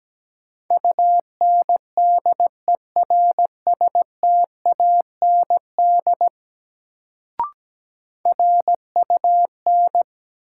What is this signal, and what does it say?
Morse code, heard as a keyed tone, about 17 wpm, 700 Hz. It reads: UNDERSTAND RUN
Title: Sets of 4 Words - New Word Drill - 17wpm